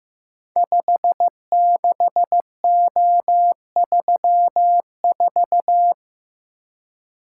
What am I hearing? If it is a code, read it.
Morse code, heard as a keyed tone, about 15 wpm, 700 Hz. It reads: 56O34
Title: Code Groups - 5 Letter:Numbers - 15wpm